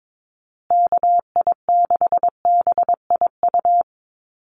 Morse code, heard as a keyed tone, about 22 wpm, 700 Hz. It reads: KI6BIU
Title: USA Call Signs Repeated 3x 22wpm